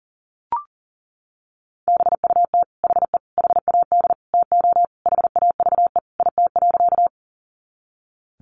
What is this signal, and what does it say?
Morse code, heard as a keyed tone, about 40 wpm, 700 Hz. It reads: BUT HE HAD TO HAVE IT.
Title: Sentences from Top 100 Words 40wpm